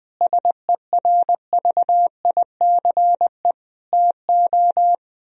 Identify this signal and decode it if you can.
Morse code, heard as a keyed tone, about 20 wpm, 700 Hz. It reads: SERVICE TO
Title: Sentences from Top 1000 Words - Review Part 4 - 20wpm